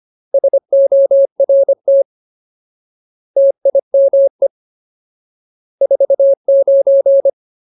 Morse code, heard as a keyed tone, about 25 wpm, 550 Hz. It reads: SORT TIME 49
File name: Ultimate Word List / 25WPM Ultimate